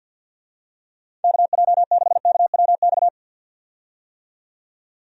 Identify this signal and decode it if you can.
Morse code, heard as a keyed tone, about 46 wpm, 700 Hz. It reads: KJ6KWX
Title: USA Call Signs - 6 Characters - 50wpm